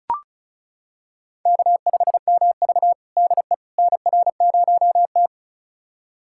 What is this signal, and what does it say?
Morse code, heard as a keyed tone, about 36 wpm, 700 Hz. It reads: K5MV DE NR0T
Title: USA Callsign DE Callsign - 4 Characters - 35wpm